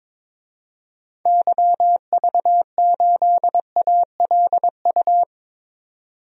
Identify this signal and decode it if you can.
Morse code, heard as a keyed tone, about 21 wpm, 700 Hz. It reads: YV8ALU
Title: DXCC-100 Call Signs - 6 Characters - 22wpm